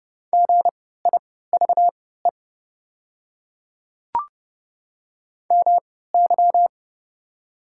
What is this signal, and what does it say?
Morse code, heard as a keyed tone, about 30 wpm, spaced as for 17 wpm, 690 Hz. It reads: GIVE MY
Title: 3x Character Spacing - Top 200 Words - 30wpm